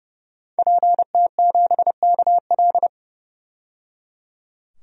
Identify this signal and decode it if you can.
Morse code, heard as a keyed tone, about 30 wpm, 700 Hz. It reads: PT7KL
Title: DXCC-100 Call Signs - 5 Characters - 30wpm